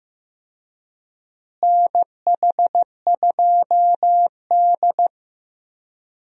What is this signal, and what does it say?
Morse code, heard as a keyed tone, about 15 wpm, 700 Hz. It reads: NH2D
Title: USA Call Signs - 4 Characters - Repeated 3x - 15wpm